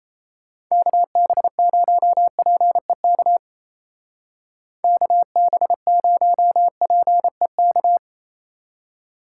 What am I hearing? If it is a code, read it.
Morse code, heard as a keyed tone, about 34 wpm, 700 Hz. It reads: KB0PEK KB0PEK
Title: Speed Racing - USA Call Signs 28wpm